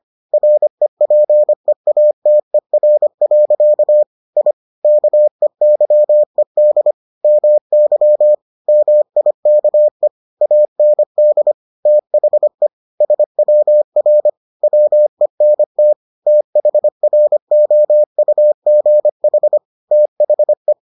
Morse code, heard as a keyed tone, about 25 wpm, 600 Hz. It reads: REPEATER. I KEYED MY MIKE AND THE SWR WENT THROUGH THE